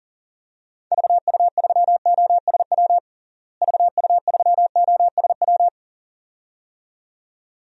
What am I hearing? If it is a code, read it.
Morse code, heard as a keyed tone, about 40 wpm, 700 Hz. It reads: VU3OSW VU3OSW
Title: DXCC-100 Call Signs - 6 Characters - Repeated 2x - 40wpm